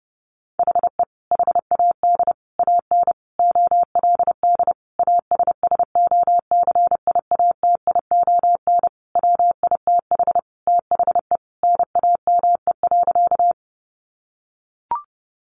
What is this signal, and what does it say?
Morse code, heard as a keyed tone, about 30 wpm, 700 Hz. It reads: HE HAD AN OLD ASSOCIATION WITH THE NAME.
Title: Sentences from Top 1100 Words 30wpm